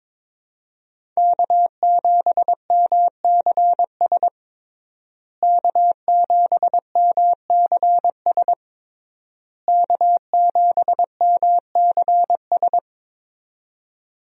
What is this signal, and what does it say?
Morse code, heard as a keyed tone, about 22 wpm, 700 Hz. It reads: K7MCS K7MCS K7MCS
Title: USA Call Signs - 5 Characters - Repeated 3x - 22wpm